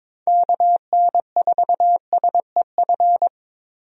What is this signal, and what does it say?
Morse code, heard as a keyed tone, about 21 wpm, 700 Hz. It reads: KN4SEF
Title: USA Call Signs - 6 Characters - Repeated 2x - 22wpm